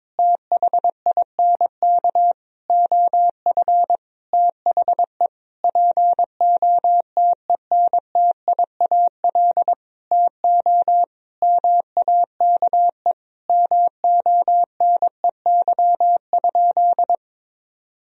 Morse code, heard as a keyed tone, about 21 wpm, 700 Hz. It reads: THINK OF THE POTENTIAL TO MAKE MONEY?
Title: Sentences from Top 1000 Words - Review Part 4 - 22wpm